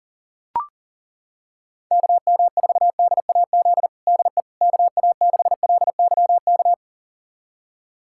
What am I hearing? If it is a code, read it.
Morse code, heard as a keyed tone, about 40 wpm, 700 Hz. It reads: KM4DAZ DE KA6LYK
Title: USA Callsign DE Callsign 40wpm